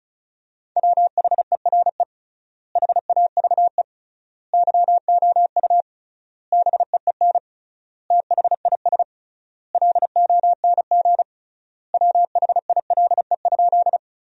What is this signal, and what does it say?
Morse code, heard as a keyed tone, about 34 wpm, 700 Hz. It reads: WHERE HAVE YOU BEEN THIS LONG WHILE?
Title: 3x Word-Spacing - Sentences from Top 300 Words - 35wpm